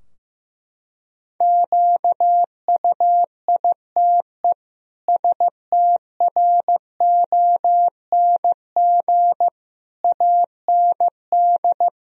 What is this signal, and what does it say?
Morse code, heard as a keyed tone, about 15 wpm, 700 Hz. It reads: QUITE STRONG AND
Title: Sentences from Top 1000 Words - Review Part 3 - 15wpm